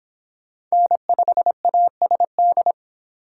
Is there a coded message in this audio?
Yes: Morse code from 0.7 to 2.7 s, heard as a keyed tone, about 25 wpm, 700 Hz. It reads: N5ASD